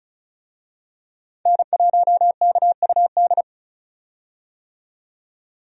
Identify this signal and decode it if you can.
Morse code, heard as a keyed tone, about 34 wpm, 700 Hz. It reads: N1KUD
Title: USA Call Signs - 5 Characters - 35wpm